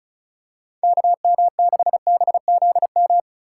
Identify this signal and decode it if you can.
Morse code, heard as a keyed tone, about 34 wpm, 690 Hz. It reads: KM6BZM
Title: USA Call Signs - 4x Character Spacing - 35wpm